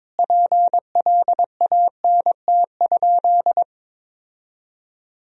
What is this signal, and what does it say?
Morse code, heard as a keyed tone, about 22 wpm, 700 Hz. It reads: PLANT?
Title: Sentences from Top 1000 Words - Review Part 1 - 22wpm